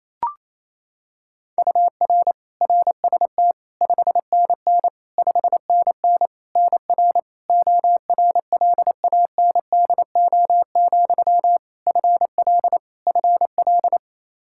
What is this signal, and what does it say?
Morse code, heard as a keyed tone, about 28 wpm, 700 Hz. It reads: UR RST 5NN 5NN NR ORLANDO, FL FL
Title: Practice QSO - Exchange 1 - Beginning - 28wpm